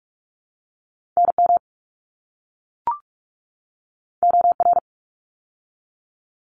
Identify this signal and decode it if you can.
Morse code, heard as a keyed tone, about 44 wpm, 700 Hz. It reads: NM OR
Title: ARRL Sections 45wpm